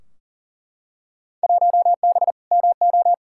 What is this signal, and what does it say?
Morse code, heard as a keyed tone, about 40 wpm, 700 Hz. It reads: 1B MO
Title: Field Day Exchanges - 40wpm